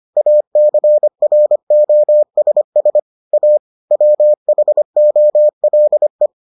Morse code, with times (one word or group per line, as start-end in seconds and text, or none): 0.2-3.0 ACROSS
3.3-3.6 A
3.9-6.3 WHOLE